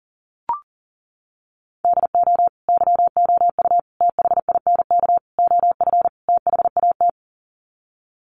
Morse code, heard as a keyed tone, about 40 wpm, 700 Hz. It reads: DO YOU THINK OF THAT
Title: Sets of 5 Words Encore2 40wpm